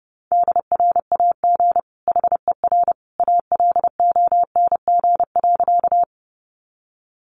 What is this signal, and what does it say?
Morse code, heard as a keyed tone, about 30 wpm, 700 Hz. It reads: DRAG HER ALONG.
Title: Sentences from Top 1500 Words 30wpm